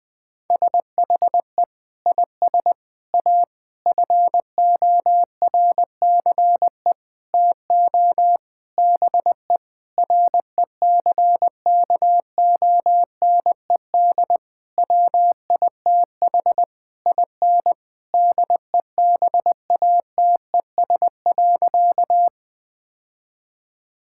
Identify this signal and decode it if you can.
Morse code, heard as a keyed tone, about 19 wpm, 700 Hz. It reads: SHE IS A FORCE TO BE RECKONED WITH IN DEBATES.
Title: Sentences with 6-Word Phrases - 20wpm